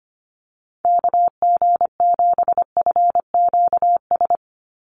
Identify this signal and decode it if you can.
Morse code, heard as a keyed tone, about 25 wpm, 700 Hz. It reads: KG7FQS
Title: USA Call Signs 25wpm